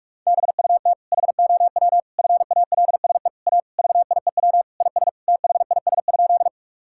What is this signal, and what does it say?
Morse code, heard as a keyed tone, about 41 wpm, 700 Hz. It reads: BUT HOW FALSE A VIEW IS THIS?
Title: Sentences from Top 1600 Words 45wpm